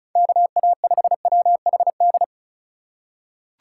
Morse code, heard as a keyed tone, about 36 wpm, 690 Hz. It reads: KA5WHD
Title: USA Call Signs - 6 Characters - 6x Character Spacing - 35wpm